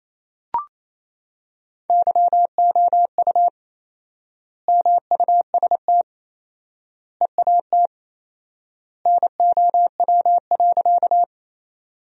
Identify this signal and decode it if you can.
Morse code, heard as a keyed tone, about 28 wpm, 700 Hz. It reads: YOU MUST EAT NOW.
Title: 4x Word-Spacing - Sentences from Top 300 Words - 28wpm